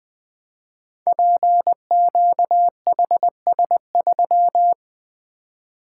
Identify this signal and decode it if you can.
Morse code, heard as a keyed tone, about 20 wpm, 700 Hz. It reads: PQHS3
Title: Code Groups - 5 Letter:Numbers - 20wpm